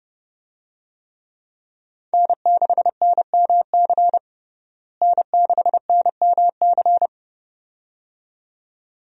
Morse code, heard as a keyed tone, about 30 wpm, 700 Hz. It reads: N6NMC N6NMC
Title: USA Call Signs - 5 Characters - Repeated 2x - 30wpm